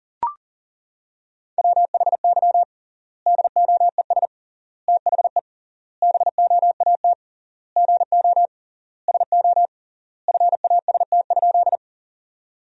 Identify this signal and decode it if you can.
Morse code, heard as a keyed tone, about 40 wpm, 700 Hz. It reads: WHY DOES THE BOAT GO SO FAST?
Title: 3x Word-Spacing - Sentences from Top 500 Words - 40wpm